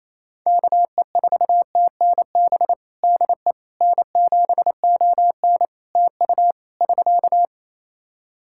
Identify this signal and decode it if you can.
Morse code, heard as a keyed tone, about 28 wpm, 700 Hz. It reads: KE4TNB DE N7ON TU <SK>
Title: Practice QSO - Exchange 4 - Ending - 28wpm